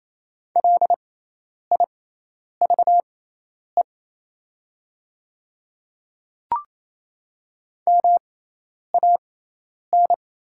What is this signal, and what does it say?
Morse code, heard as a keyed tone, about 28 wpm, spaced as for 10 wpm, 700 Hz. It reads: LIVE MAN
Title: 6x Character Spacing - Top 200 Words - 28wpm